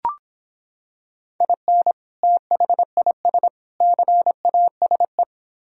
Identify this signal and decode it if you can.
Morse code, heard as a keyed tone, about 26 wpm, 700 Hz. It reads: IN THIS CASE